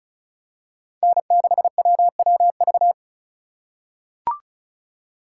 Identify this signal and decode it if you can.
Morse code, heard as a keyed tone, about 36 wpm, 700 Hz. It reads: N6WWV
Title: USA Call Signs - 5 Characters - 4x Character Spacing - 35wpm